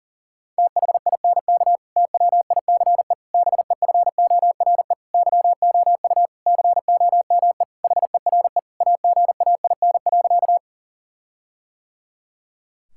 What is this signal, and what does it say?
Morse code, heard as a keyed tone, about 40 wpm, 700 Hz. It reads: THINK TWICE BEFORE YOU COME HERE AGAIN.
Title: Sentences from Top 1600 Words 40wpm